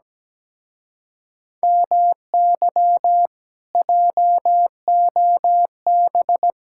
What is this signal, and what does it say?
Morse code, heard as a keyed tone, about 17 wpm, 700 Hz. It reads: MY JOB
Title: Speed-Racing Common QSO Phrases Encore - 17wpm